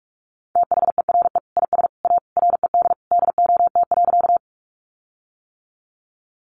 Morse code, heard as a keyed tone, about 44 wpm, 700 Hz. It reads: THERE IS A RED DOT.
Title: Sentences from Top 1500 Words 45wpm